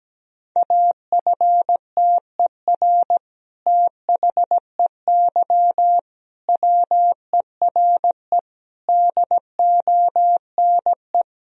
Morse code, heard as a keyed tone, about 17 wpm, 700 Hz. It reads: AFTER THEY WERE DONE
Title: Sentences from Top 1000 Words - Review Part 2 - 17wpm